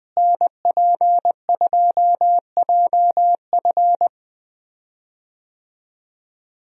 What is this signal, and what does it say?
Morse code, heard as a keyed tone, about 20 wpm, 700 Hz. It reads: NP2JF
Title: USA Call Signs - 5 Characters - Repeated 2x - 20wpm